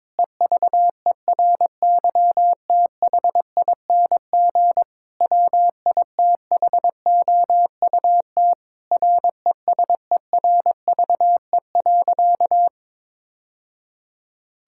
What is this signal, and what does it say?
Morse code, heard as a keyed tone, about 22 wpm, 700 Hz. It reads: EVERYTHING WITHOUT RESERVE.
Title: Sentences from Top 1000 Words - Review Part 6 - 22wpm